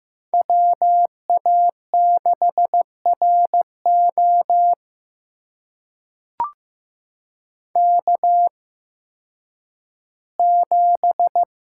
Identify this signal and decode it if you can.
Morse code, heard as a keyed tone, about 15 wpm, spaced as for 10 wpm, 700 Hz. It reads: WA6RO K7
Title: USA Call Signs - 5 Characters - 8x Character Spacing - 15wpm